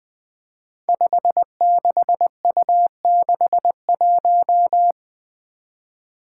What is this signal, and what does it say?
Morse code, heard as a keyed tone, about 20 wpm, 700 Hz. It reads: 56U61
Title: Five Challenging Letter-Number Combinations 20wpm